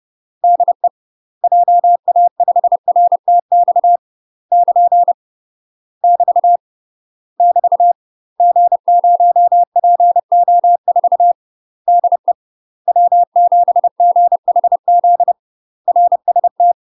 Morse code, heard as a keyed tone, about 30 wpm, 700 Hz. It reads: DE JA5RTX <KN> <BT> <BT> G0PO4 DE W7GHZ RST
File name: UK MASTER QSOs TRAINING / 30WPM / UK MASTER QSOs 0047